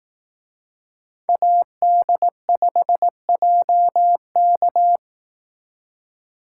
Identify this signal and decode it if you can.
Morse code, heard as a keyed tone, about 18 wpm, 700 Hz. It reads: AD5JK
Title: Speed Racing - USA 2x2 Call Signs 15wpm